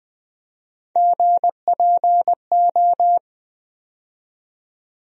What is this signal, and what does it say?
Morse code, heard as a keyed tone, about 21 wpm, 700 Hz. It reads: GPO
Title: USA Call Sign Postfixes - 3 Letters 20wpm